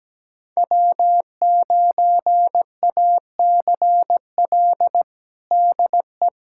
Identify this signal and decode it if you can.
Morse code, heard as a keyed tone, about 17 wpm, 700 Hz. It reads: W9ACL DE
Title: USA Callsign DE Callsign - 5 Characters - 17wpm